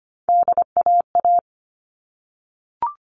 Morse code, heard as a keyed tone, about 25 wpm, 700 Hz. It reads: DAA
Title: All Cut Numbers - 3-Digit 25wpm